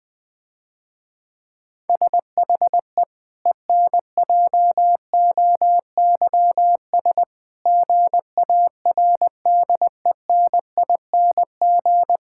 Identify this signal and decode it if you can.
Morse code, heard as a keyed tone, about 20 wpm, 700 Hz. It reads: SHE ENJOYS GARDENING
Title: Difficult Sentences with Hints - 20wpm